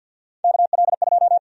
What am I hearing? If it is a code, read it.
Morse code, heard as a keyed tone, about 52 wpm, 700 Hz. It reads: KL2
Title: USA Call Sign Prefixes - 50wpm